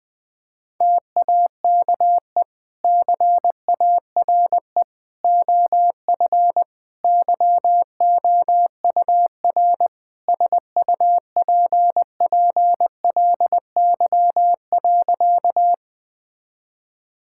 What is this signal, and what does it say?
Morse code, heard as a keyed tone, about 21 wpm, 700 Hz. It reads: TAKE CARE OF YOUR SUPPLY.
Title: Sentences from Top 1000 Words - Review Part 5 - 20wpm